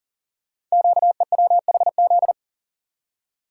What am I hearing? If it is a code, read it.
Morse code, heard as a keyed tone, about 39 wpm, 700 Hz. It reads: QEWHZ
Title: Code Groups - 5 Letter:Numbers - 40wpm